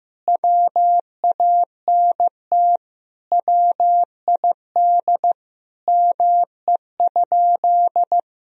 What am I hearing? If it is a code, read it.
Morse code, heard as a keyed tone, about 15 wpm, 700 Hz. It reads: WANT WID ME?
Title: Sentences from Top 200 Words and Abbreviations 15wpm